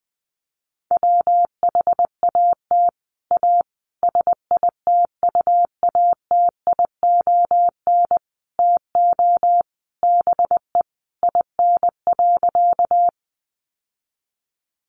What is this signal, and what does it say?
Morse code, heard as a keyed tone, about 19 wpm, 700 Hz. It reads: WHAT A SITUATION TO BE IN.
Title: Sentences from Top 1200 Words 20wpm